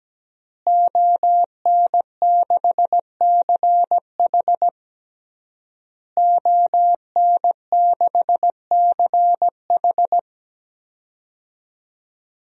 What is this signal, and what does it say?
Morse code, heard as a keyed tone, about 17 wpm, 700 Hz. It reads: ON6CH ON6CH
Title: DXCC-100 Call Signs - 5 Characters - Repeated 3x - 17wpm